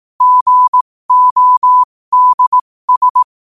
Morse code, heard as a keyed tone, about 18 wpm, spaced as for 15 wpm, 1000 Hz. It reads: GODS
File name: qotdfiles / 15wpm